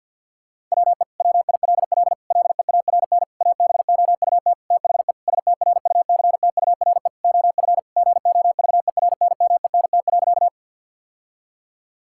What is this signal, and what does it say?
Morse code, heard as a keyed tone, about 50 wpm, 700 Hz. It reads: WE WILL LEARN ABOUT THE STRUCTURE OF GOVERNMENT.
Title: Sentences from Top 1000 Words - Review Part 6 - 50wpm